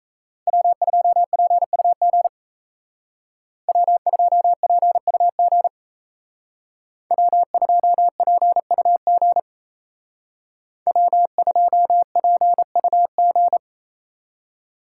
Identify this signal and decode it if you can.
Morse code, heard as a keyed tone, about 35 wpm, 700 Hz. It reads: W2PUG W2PUG W2PUG W2PUG
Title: Speed Racing - USA Call Signs 28wpm